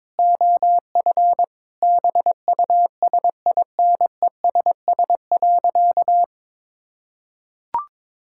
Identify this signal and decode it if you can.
Morse code, heard as a keyed tone, about 22 wpm, 700 Hz. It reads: OF BUSINESS.
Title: Sentences from Top 1000 Words - Review Part 6 - 22wpm